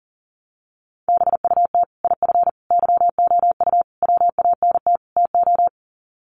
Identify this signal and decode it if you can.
Morse code, heard as a keyed tone, about 40 wpm, 700 Hz. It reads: BUT IF YOU WANT TO
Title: Sets of 5 Words Encore2 40wpm